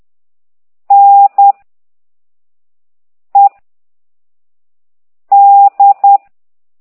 Morse code, heard as a keyed tone, about 10 wpm, 800 Hz. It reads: N E D
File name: ct2idw / exerc9